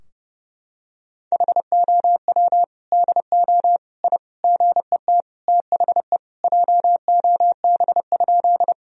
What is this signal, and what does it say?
Morse code, heard as a keyed tone, about 30 wpm, 700 Hz. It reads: HOW DO I GET THE JOB?
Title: Sentences from Top 1000 Words - Review Part 3 - 30wpm